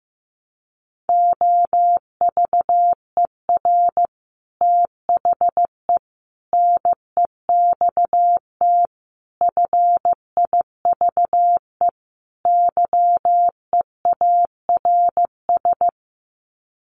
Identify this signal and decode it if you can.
Morse code, heard as a keyed tone, about 15 wpm, 700 Hz. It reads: OVER THE NEXT FIVE YEARS
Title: Sets of 5 Words 15wpm